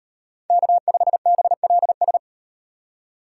Morse code, heard as a keyed tone, about 38 wpm, 700 Hz. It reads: K5BLS